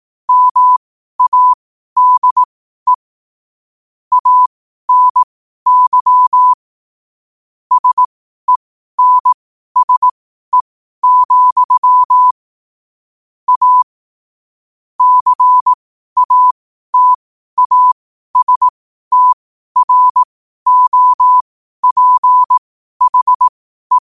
Morse code, heard as a keyed tone, about 18 wpm, spaced as for 12 wpm, 1000 Hz. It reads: MADE ANY SENSE, A CATASTROPHE
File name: qotdfiles / 12wpm